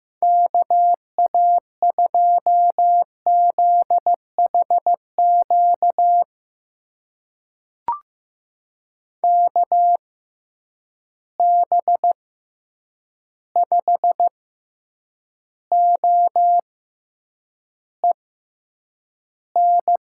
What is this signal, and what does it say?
Morse code, heard as a keyed tone, about 15 wpm, spaced as for 8 wpm, 700 Hz. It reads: KA2ZHQ KB5OEN
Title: USA Call Signs - 6 Characters - 6x Character Spacing - 15wpm